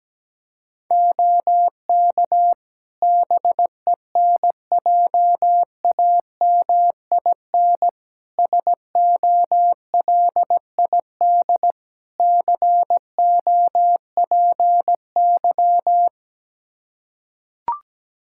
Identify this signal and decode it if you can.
Morse code, heard as a keyed tone, about 17 wpm, 700 Hz. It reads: OK BENJAMIN SOLID COPY
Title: Practice QSO - Exchange 3 - More Info - 17wpm